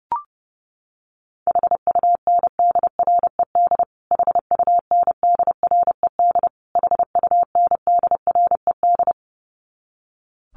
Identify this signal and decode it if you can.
Morse code, heard as a keyed tone, about 30 wpm, 700 Hz. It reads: HUNDRED HUNDRED HUNDRED
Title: Top 500 Words Repeated 30wpm